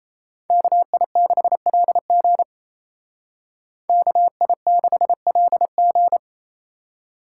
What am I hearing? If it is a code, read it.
Morse code, heard as a keyed tone, about 36 wpm, 700 Hz. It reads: KI6LG KI6LG